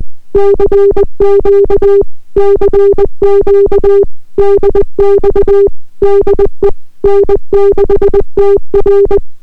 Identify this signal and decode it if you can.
Morse code, heard as a keyed tone, about 20 wpm, 400 Hz. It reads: CQ CQ DX DE N6TR